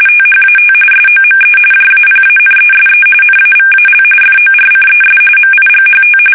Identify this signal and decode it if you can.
RTTY, about 45.45 baud. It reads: THAT SOME JAPANESE NAVAL FORCES MIGHT
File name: fsk850 / battleshipnj